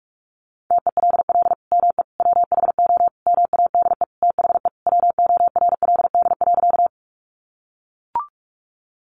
Morse code, heard as a keyed tone, about 44 wpm, 700 Hz. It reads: TELL ME WHO MADE THE WORLD.
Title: Sentences from Top 200 Words 45wpm